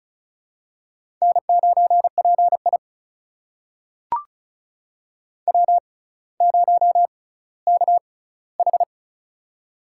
Morse code, heard as a keyed tone, about 36 wpm, spaced as for 18 wpm, 700 Hz. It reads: N9PI W0KH
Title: USA Call Signs - 4 Characters - 6x Character Spacing - 35wpm